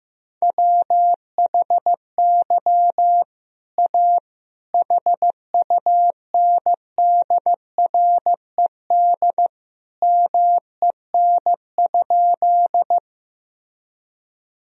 Morse code, heard as a keyed tone, about 15 wpm, 700 Hz. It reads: WHY A HUNDRED MEN?
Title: Sentences from Top 1000 Words - Review Part 2 - 15wpm